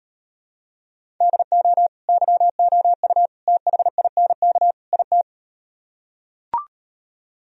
Morse code, heard as a keyed tone, about 38 wpm, 700 Hz. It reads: DO YOU THINK IT